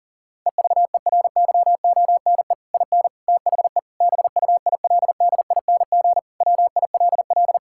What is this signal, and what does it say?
Morse code, heard as a keyed tone, about 40 wpm, 700 Hz. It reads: EVERYONE IN THE BUILDING WILL
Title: Sentences from Top 1000 Words - Review Part 6 - 40wpm